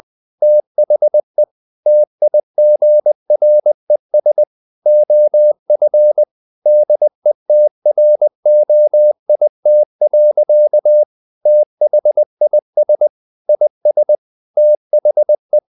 Morse code, heard as a keyed tone, about 20 wpm, 600 Hz. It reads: THE TIGRES OF DETROIT. THIS IS THE
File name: the old man and the sea / the old man and the sea part 1 20wpm